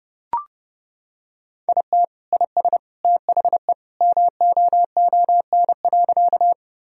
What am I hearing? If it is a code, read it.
Morse code, heard as a keyed tone, about 32 wpm, 700 Hz. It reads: IT IS THE MOON.
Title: Sentences from Top 1000 Words - Review Part 3 - 30wpm